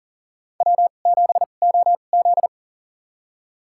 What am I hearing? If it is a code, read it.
Morse code, heard as a keyed tone, about 40 wpm, spaced as for 29 wpm, 700 Hz. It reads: W7OZ
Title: USA Call Signs - 4 Characters - 2x Character Spacing - 40wpm